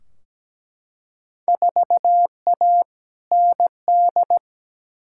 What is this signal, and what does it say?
Morse code, heard as a keyed tone, about 17 wpm, 700 Hz. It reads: 4A ND
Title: Field Day Exchanges Encore - 17wpm